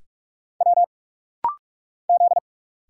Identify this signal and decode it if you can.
Morse code, heard as a keyed tone, about 43 wpm, 700 Hz. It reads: W Z
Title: Single Letter-Number - Rapid-Fire 45wpm